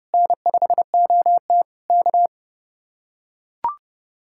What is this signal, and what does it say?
Morse code, heard as a keyed tone, about 32 wpm, 700 Hz. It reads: N5OT K